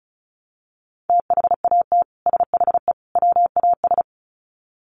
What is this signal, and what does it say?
Morse code, heard as a keyed tone, about 34 wpm, 700 Hz. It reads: THAT SHE WAS
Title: Sets of 3 Words 35wpm